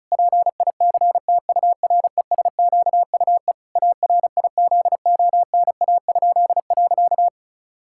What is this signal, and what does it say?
Morse code, heard as a keyed tone, about 34 wpm, 700 Hz. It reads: PICTURESQUE ARIZONA?.
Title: Pangram Challenge v2 - 35wpm